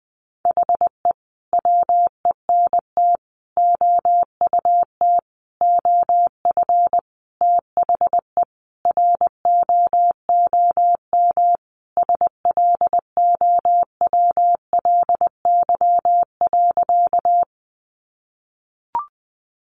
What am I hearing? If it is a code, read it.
Morse code, heard as a keyed tone, about 20 wpm, 700 Hz. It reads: HE WENT OUT OF THE ROOM SLOWLY.
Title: Sentences from Top 1100 Words 20wpm